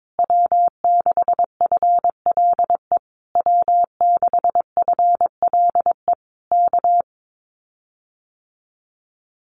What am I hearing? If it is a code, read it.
Morse code, heard as a keyed tone, about 22 wpm, 700 Hz. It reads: W6FLE W6FLE K
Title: Common QSO Phrases Encore 22wpm